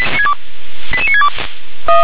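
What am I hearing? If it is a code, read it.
Five-tone selective call (ZVEI), heard as quick runs of five tones, tones of about 70 ms. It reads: 88962
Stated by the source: ZVEI five-tone call received by SDR.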